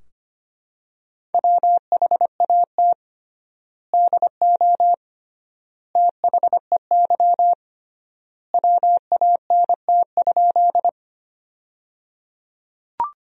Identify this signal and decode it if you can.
Morse code, heard as a keyed tone, about 25 wpm, 700 Hz. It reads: WHAT DO THEY WANT?
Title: 3x Word-Spacing - Sentences from Top 200 Words - 25wpm